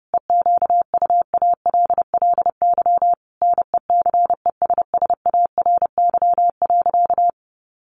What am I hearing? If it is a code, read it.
Morse code, heard as a keyed tone, about 30 wpm, 700 Hz. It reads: EQUALLY NECESSARY.
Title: Sentences from Top 1400 Words 30wpm